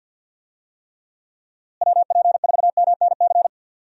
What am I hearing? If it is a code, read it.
Morse code, heard as a keyed tone, about 51 wpm, 700 Hz. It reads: WP4MNC
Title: USA Call Signs - 6 Characters - Repeated 2x - 50wpm